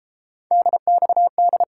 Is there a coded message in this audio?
Yes: Morse code from 0.5 to 1.6 s, heard as a keyed tone, about 32 wpm, 700 Hz. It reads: DXD